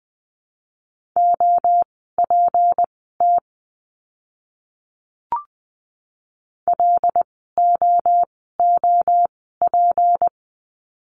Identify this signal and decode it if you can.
Morse code, heard as a keyed tone, about 20 wpm, spaced as for 15 wpm, 700 Hz. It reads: OPT LOOP
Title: 153 Lesson 25f - Letter p - Words 20wpm